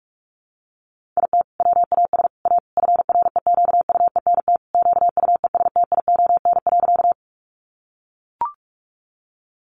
Morse code, heard as a keyed tone, about 44 wpm, 700 Hz. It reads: IT WAS A FREQUENT QUESTION.
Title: Sentences from Top 1400 Words 45wpm